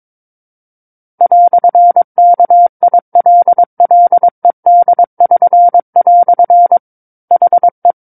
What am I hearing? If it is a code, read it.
Morse code, heard as a keyed tone, about 22 wpm, 700 Hz. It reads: "KILLED<SN>" HE
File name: Book One / 13-Chapter 13 - 22 WPM